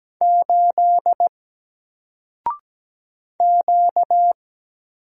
Morse code, heard as a keyed tone, about 17 wpm, 700 Hz. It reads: 8 Q
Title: Single Letter-Number Alt 17wpm